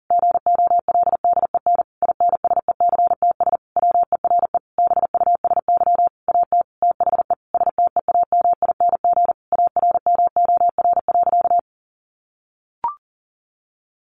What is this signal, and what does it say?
Morse code, heard as a keyed tone, about 37 wpm, 700 Hz. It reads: GOLDEN INSECTS WERE BUSY AT THE STEAMING ARMOR.
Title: A Sound Of Thunder 40wpm